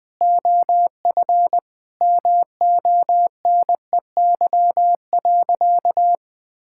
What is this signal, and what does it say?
Morse code, heard as a keyed tone, about 20 wpm, 700 Hz. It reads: OF MONEY.
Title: Sentences from Top 1000 Words - Review Part 6 - 20wpm